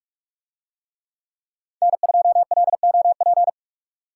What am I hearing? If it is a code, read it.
Morse code, heard as a keyed tone, about 46 wpm, 700 Hz. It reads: N2LOP